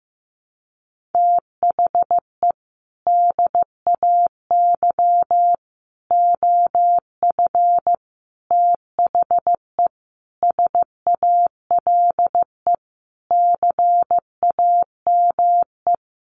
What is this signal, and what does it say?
Morse code, heard as a keyed tone, about 15 wpm, 700 Hz. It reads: THE DAY OF THE SALE CAME
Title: Sentences from Top 1400 Words 15wpm